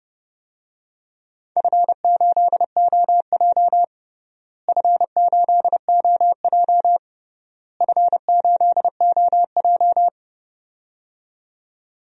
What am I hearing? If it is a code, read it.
Morse code, heard as a keyed tone, about 30 wpm, 700 Hz. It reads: F8OJ F8OJ F8OJ
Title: DXCC-100 Call Signs - 4 Characters - Repeated 3x - 30wpm